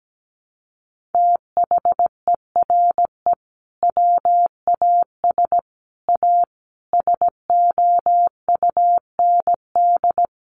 Morse code, heard as a keyed tone, about 17 wpm, 700 Hz. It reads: THERE WAS A SOUND
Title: A Sound Of Thunder 17wpm